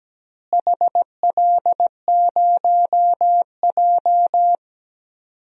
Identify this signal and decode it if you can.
Morse code, heard as a keyed tone, about 17 wpm, 700 Hz. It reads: HL0J
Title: DXCC-100 Call Signs - 4 Characters - Repeated 3x - 17wpm